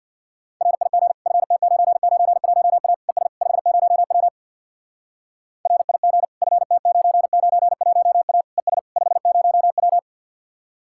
Morse code, heard as a keyed tone, about 63 wpm, 700 Hz. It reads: RIG FT991A ES 50W RIG FT991A ES 50W